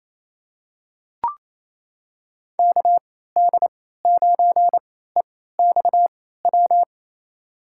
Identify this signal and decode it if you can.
Morse code, heard as a keyed tone, about 28 wpm, spaced as for 16 wpm, 700 Hz. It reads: KD9EXW
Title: USA Call Signs - 6 Characters - 3x Character Spacing - 28wpm